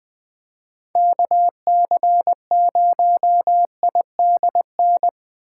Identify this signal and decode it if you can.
Morse code, heard as a keyed tone, about 23 wpm, 700 Hz. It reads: KC0IDN
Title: Speed Racing - USA 2x3 Call Signs 20wpm